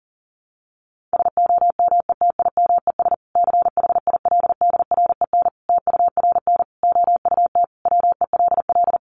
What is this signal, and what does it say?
Morse code, heard as a keyed tone, about 40 wpm, 700 Hz. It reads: SOMETIMES CHILDREN TURN OUT WELL
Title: Sentences from Top 600 Words 40wpm